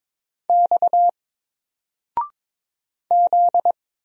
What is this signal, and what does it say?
Morse code, heard as a keyed tone, about 22 wpm, 700 Hz. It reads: X Z
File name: Single-Letter-Number-22wpm